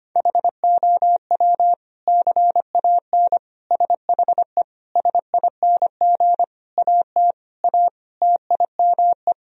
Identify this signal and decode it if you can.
Morse code, heard as a keyed tone, about 25 wpm, 700 Hz. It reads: HOW CAN SHE SING AT A TIME
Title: Sentences from Top 1000 Words - Review Part 2 - 25wpm